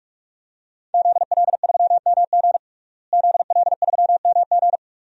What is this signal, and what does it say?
Morse code, heard as a keyed tone, about 46 wpm, 700 Hz. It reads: ZL3MG ZL3MG
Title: DXCC-100 Call Signs - 5 Characters - Repeated 2x - 45wpm